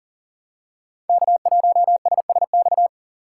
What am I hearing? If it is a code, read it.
Morse code, heard as a keyed tone, about 41 wpm, 700 Hz. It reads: K1SSX
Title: Code Groups - 5 Letter:Numbers - 40wpm